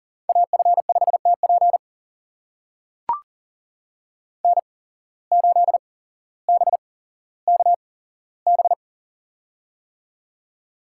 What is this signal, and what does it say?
Morse code, heard as a keyed tone, about 40 wpm, spaced as for 17 wpm, 700 Hz. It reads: AF5TP N8BKB
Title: USA Call Signs - 5 Characters - 8x Character Spacing - 40wpm